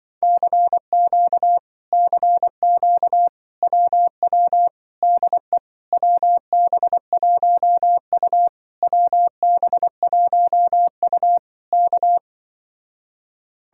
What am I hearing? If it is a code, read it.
Morse code, heard as a keyed tone, about 24 wpm, 700 Hz. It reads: CQ CQ WW DE WB1U WB1U K
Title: Speed-Racing Common QSO Phrases Encore - 20wpm